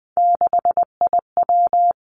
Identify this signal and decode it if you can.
Morse code, heard as a keyed tone, about 19 wpm, 700 Hz. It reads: 6IW
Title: Three Challenging Letter-Number Combinations 20wpm